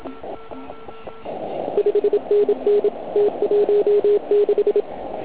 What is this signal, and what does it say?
Morse code, heard as a keyed tone, about 27 wpm, 410 Hz. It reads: 5NN T16
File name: Archiv / CUC129 / CUC129 pileup